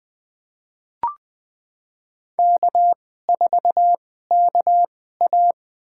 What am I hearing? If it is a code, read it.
Morse code, heard as a keyed tone, about 20 wpm, spaced as for 14 wpm, 700 Hz. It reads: K4KA
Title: USA Call Signs - 4 Characters - 2x Character Spacing - 20wpm